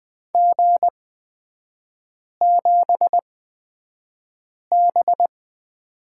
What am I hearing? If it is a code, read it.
Morse code, heard as a keyed tone, about 20 wpm, spaced as for 5 wpm, 700 Hz. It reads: G7B
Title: 224 Lesson 36e - BT - Three Characters 20wpm